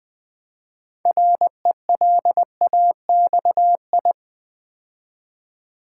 Morse code, heard as a keyed tone, about 20 wpm, 700 Hz. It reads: RELAXI
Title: Build A Word - 3 Letters At A Time - 8-Letter Words - 20wpm